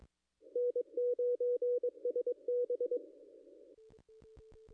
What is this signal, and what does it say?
Morse code, heard as a keyed tone, about 22 wpm, 490 Hz. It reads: N9SB
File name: contests / ss-cw-k5tr-op-n5rz -0100